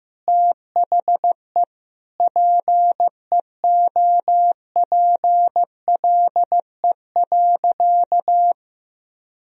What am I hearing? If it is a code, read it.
Morse code, heard as a keyed tone, about 15 wpm, 700 Hz. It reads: THE PEOPLE.
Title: Sentences from Top 400 Words and Abbreviations 15wpm